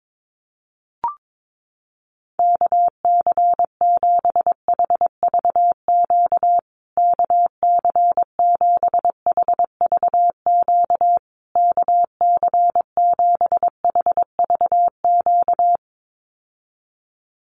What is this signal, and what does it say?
Morse code, heard as a keyed tone, about 22 wpm, 700 Hz. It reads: KC7HVQ KC7HVQ KC7HVQ
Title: USA Call Signs Repeated 3x 22wpm